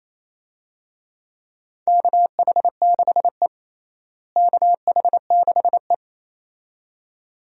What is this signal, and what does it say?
Morse code, heard as a keyed tone, about 28 wpm, 700 Hz. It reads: KH6E KH6E
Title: DXCC-100 Call Signs - 4 Characters - Repeated 2x - 28wpm